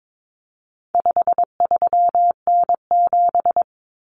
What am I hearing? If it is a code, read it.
Morse code, heard as a keyed tone, about 22 wpm, 700 Hz. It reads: 53N7
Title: Common Cut Numbers - 4-Digit 22wpm